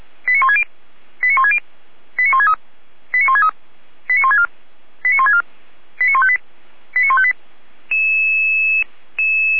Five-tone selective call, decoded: 89160 89160 89163 89163 89164 89164 89158 89158

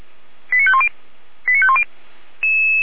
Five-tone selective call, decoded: 89510 89510

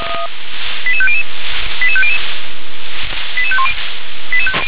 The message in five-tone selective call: 88500 88500 88510

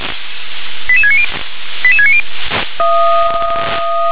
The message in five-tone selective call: 88690 88690

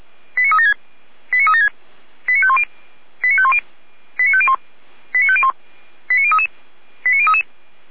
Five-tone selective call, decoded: 89376 89376 89510 89510 89501 89501 89033 89033

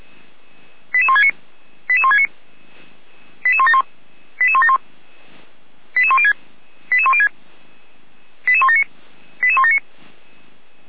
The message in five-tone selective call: 88169 88169 88171 88171 88106 88106 88179 88179

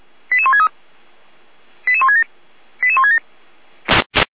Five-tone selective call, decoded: 88163 88167 88167

